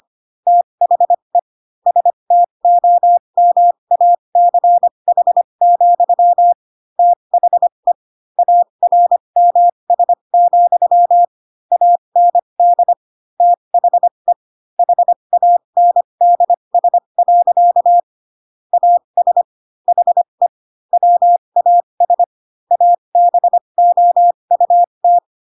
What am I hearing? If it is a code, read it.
Morse code, heard as a keyed tone, about 25 wpm, spaced as for 22 wpm, 700 Hz. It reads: THE STOMACH, THE ARMS, AND THE HANDS. AS HE WAS ABOUT
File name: Pinocchio / Pinocchio ch3 18 25 ik1hge